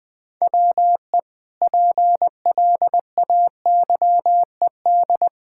Morse code, heard as a keyed tone, about 20 wpm, 700 Hz. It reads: WE PLAYED
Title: Sentences with 5-Word Phrases - Spoken Beginning - 20wpm